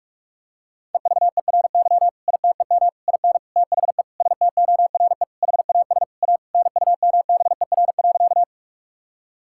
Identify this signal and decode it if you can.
Morse code, heard as a keyed tone, about 42 wpm, 700 Hz. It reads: EVERY ITEM IN THE STORE HAS A NUMBER.
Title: Sentences from Top 1000 Words - Review Part 6 - 45wpm